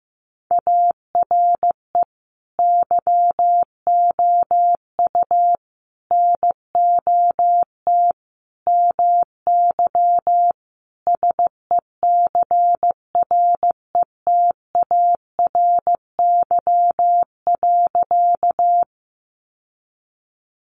Morse code, heard as a keyed tone, about 14 wpm, 700 Hz. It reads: ARE YOU NOT MY SECRETARY.
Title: Sentences from Top 1300 Words 15wpm